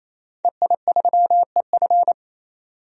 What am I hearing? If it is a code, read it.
Morse code, heard as a keyed tone, about 27 wpm, 700 Hz. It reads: EI3EF
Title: DXCC-100 Call Signs - 5 Characters - 28wpm